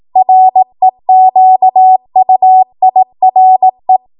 Morse code, heard as a keyed tone, about 18 wpm, 750 Hz. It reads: REQUIRE
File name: Morse / 18WPM